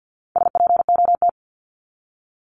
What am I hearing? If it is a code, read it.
Morse code, heard as a keyed tone, about 50 wpm, 700 Hz. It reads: SPOT
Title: Top 1001-1250 Words 50wpm